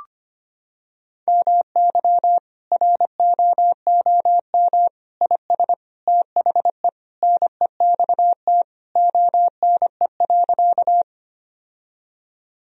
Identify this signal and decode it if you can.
Morse code, heard as a keyed tone, about 25 wpm, 700 Hz. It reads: MY ROOM IS THE NEXT ONE.
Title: Sentences from Top 1000 Words - Review Part 1 - 25wpm